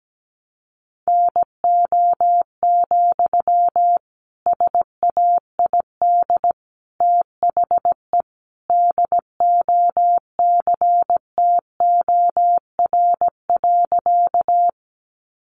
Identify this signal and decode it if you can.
Morse code, heard as a keyed tone, about 17 wpm, 700 Hz. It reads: NO, SAID THE DOCTOR.
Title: Sentences from Top 1500 Words 17wpm